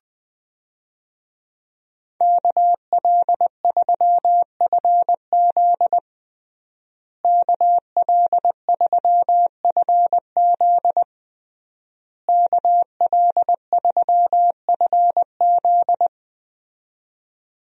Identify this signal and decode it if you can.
Morse code, heard as a keyed tone, about 20 wpm, 700 Hz. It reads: KL3FZ KL3FZ KL3FZ
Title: USA Call Signs - 5 Characters - Repeated 3x - 20wpm